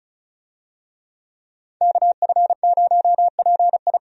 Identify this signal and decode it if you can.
Morse code, heard as a keyed tone, about 36 wpm, 700 Hz. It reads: KF0PI
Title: USA Call Signs - 5 Characters - 35wpm